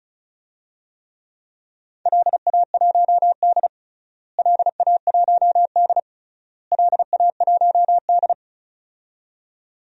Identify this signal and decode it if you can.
Morse code, heard as a keyed tone, about 36 wpm, 700 Hz. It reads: LA1D LA1D LA1D
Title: DXCC-100 Call Signs - 4 Characters - Repeated 3x - 35wpm